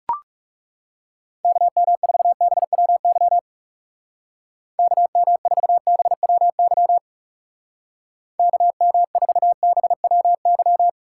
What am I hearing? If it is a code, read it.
Morse code, heard as a keyed tone, about 37 wpm, 700 Hz. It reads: KM4BWY KM4BWY KM4BWY
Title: Speed Racing - USA 2x3 Call Signs 30wpm